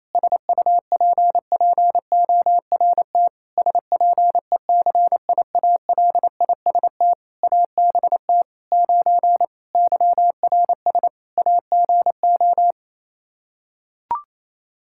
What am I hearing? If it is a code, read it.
Morse code, heard as a keyed tone, about 27 wpm, 700 Hz. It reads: SUPPORT SPECIALIST ABT 9 YRS AGO
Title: Practice QSO - Exchange 3 - More Info - 28wpm